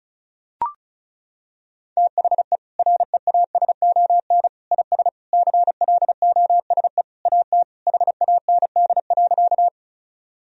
Morse code, heard as a keyed tone, about 36 wpm, 700 Hz. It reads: THE REASON IS CLOSE AT HAND.
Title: Sentences from Top 1000 Words - Review Part 1 - 35wpm